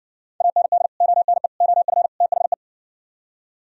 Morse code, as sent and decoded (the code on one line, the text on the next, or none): .- -. -.. / --- -. / --- ..-. / - .... .
AND ONE OF THE